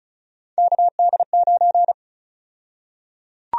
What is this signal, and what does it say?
Morse code, heard as a keyed tone, about 34 wpm, 700 Hz. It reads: KD9
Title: USA Call Sign Prefixes - 35wpm